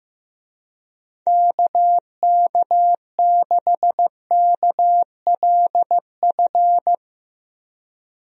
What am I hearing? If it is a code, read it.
Morse code, heard as a keyed tone, about 15 wpm, 700 Hz. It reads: KK6KLF
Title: USA Call Signs - 6 Characters - 15wpm